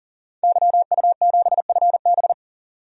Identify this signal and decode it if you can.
Morse code, heard as a keyed tone, about 40 wpm, 700 Hz. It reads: YU7FB
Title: DXCC-100 Call Signs - 5 Characters - 40wpm